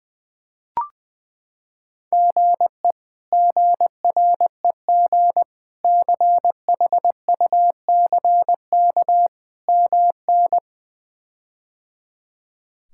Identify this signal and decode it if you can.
Morse code, heard as a keyed tone, about 20 wpm, 700 Hz. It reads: GE GREG CHUCK MN
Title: K1USN SST QSO Practice - 20wpm